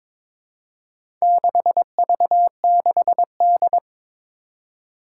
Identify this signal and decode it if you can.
Morse code, heard as a keyed tone, about 22 wpm, 700 Hz. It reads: 6V6D
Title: Four Challenging Letter-Number Combinations 22wpm